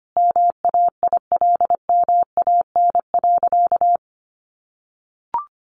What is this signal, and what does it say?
Morse code, heard as a keyed tone, about 25 wpm, 700 Hz. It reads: MAILMAN.
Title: Easy Phrases 25wpm